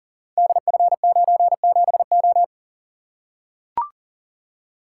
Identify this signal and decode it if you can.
Morse code, heard as a keyed tone, about 39 wpm, 700 Hz. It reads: DF97O
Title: Code Groups - 5 Letter:Numbers - 40wpm